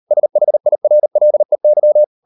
Morse code, heard as a keyed tone, about 38 wpm, 600 Hz. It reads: SHIRLEY